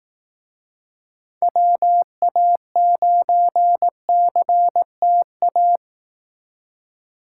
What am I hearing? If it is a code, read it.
Morse code, heard as a keyed tone, about 18 wpm, 700 Hz. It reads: WA9CTA
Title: Speed Racing - USA 2x3 Call Signs 15wpm